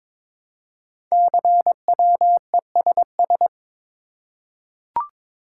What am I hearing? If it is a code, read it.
Morse code, heard as a keyed tone, about 22 wpm, 700 Hz. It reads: CWESS
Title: Code Groups - 5 Letters - 22wpm